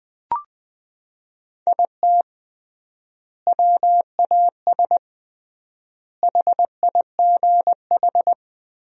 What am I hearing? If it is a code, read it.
Morse code, heard as a keyed tone, about 21 wpm, 700 Hz. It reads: IT WAS HIGH
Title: 3x Word-Spacing - Sentences from Top 200 Words - 20wpm